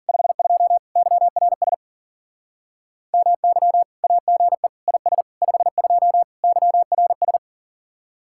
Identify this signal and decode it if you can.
Morse code, heard as a keyed tone, about 52 wpm, 700 Hz. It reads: 52 YRS MY AGE IS 52 YRS